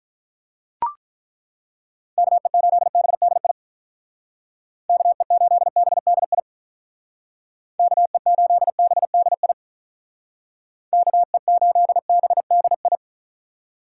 Morse code, heard as a keyed tone, about 43 wpm, 700 Hz. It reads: KE8BDI KE8BDI KE8BDI KE8BDI
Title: Speed Racing - USA Call Signs 35wpm